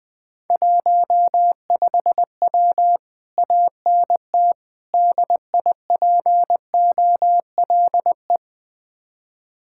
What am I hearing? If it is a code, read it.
Morse code, heard as a keyed tone, about 20 wpm, 700 Hz. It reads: 15W ANT DIPOLE
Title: Speed-Racing Common QSO Phrases - 17wpm